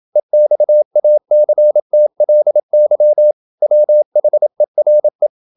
Morse code, heard as a keyed tone, about 27 wpm, 600 Hz. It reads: EXACTLY WHERE
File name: Never 0027wpm